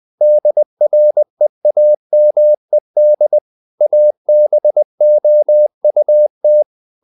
Morse code, heard as a keyed tone, about 20 wpm, 600 Hz. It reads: DREAMED ABOUT
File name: the old man and the sea / the old man and the sea part 1 20wpm